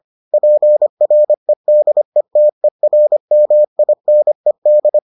Morse code, heard as a keyed tone, about 25 wpm, 600 Hz. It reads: PREDETERMINED